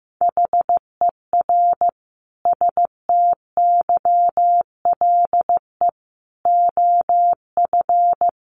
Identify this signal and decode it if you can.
Morse code, heard as a keyed tone, about 15 wpm, 700 Hz. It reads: HER STYLE OF
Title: Sentences from Top 800 Words 15wpm